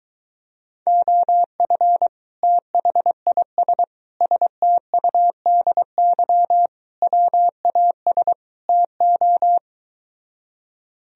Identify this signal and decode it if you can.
Morse code, heard as a keyed tone, about 22 wpm, 700 Hz. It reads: OF THIS STUDY WAS TO